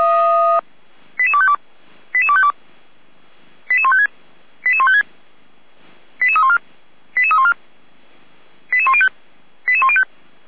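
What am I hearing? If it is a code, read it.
Five-tone selective call (ZVEI), heard as quick runs of five tones, tones of about 70 ms. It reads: 88252 88252 88156 88156 88314 88314 88105 88105
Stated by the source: ZVEI five-tone selective call received on an SDR, from the Feuerwehr Orlamünde fire brigade audio server.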